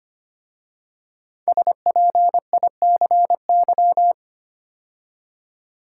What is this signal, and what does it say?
Morse code, heard as a keyed tone, about 25 wpm, 700 Hz. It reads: SPICY
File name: 5 Letter Words - First 3 Letters Given - 25wpm